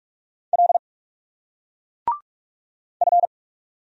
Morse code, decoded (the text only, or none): L F